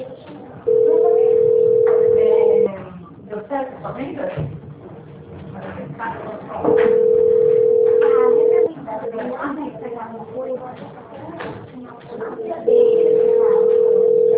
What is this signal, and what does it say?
Ringback tone (North America), 440 + 480 Hz, with speech.
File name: out- -2020- - - .26644